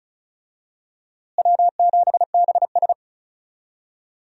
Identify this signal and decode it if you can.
Morse code, heard as a keyed tone, about 34 wpm, 700 Hz. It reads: W7BS
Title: USA Call Signs - 4 Characters - Repeated 2x - 35wpm